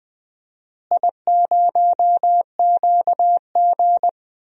Morse code, heard as a keyed tone, about 20 wpm, 700 Hz. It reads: I0QG
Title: DXCC-100 Call Signs - 4 Characters - Repeated 2x - 20wpm